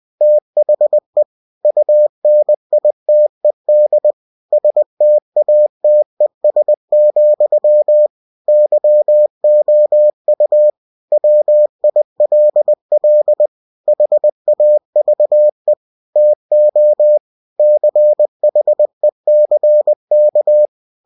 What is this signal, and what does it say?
Morse code, heard as a keyed tone, about 20 wpm, 600 Hz. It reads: THE UNITED STATES, YOU WILL HAVE TO CHECK
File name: cw-books / winnie-the-pooh / winnie-20wpm-page-0115